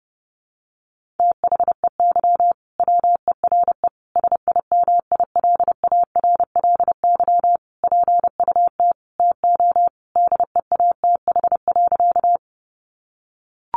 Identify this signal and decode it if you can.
Morse code, heard as a keyed tone, about 30 wpm, 700 Hz. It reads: THEY WERE SIMILARLY PUT TO DEATH.
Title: Sentences from Top 1400 Words 30wpm